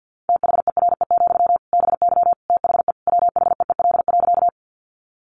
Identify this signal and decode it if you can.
Morse code, heard as a keyed tone, about 50 wpm, 700 Hz. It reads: THERE, BY THE WHEEL.
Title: Sentences from Top 500 Words 50wpm